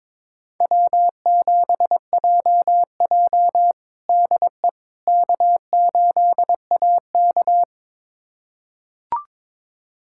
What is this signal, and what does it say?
Morse code, heard as a keyed tone, about 22 wpm, 700 Hz. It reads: W7JJ DE K8AK
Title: USA Callsign DE Callsign - 4 Characters - 22wpm